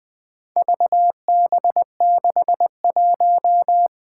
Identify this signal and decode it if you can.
Morse code, heard as a keyed tone, about 20 wpm, 700 Hz. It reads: VB61
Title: Four Challenging Letter-Number Combinations 20wpm